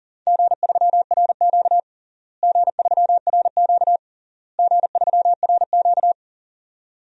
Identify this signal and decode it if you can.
Morse code, heard as a keyed tone, about 40 wpm, 700 Hz. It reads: G3RQ G3RQ G3RQ
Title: DXCC-100 Call Signs - 4 Characters - Repeated 3x - 40wpm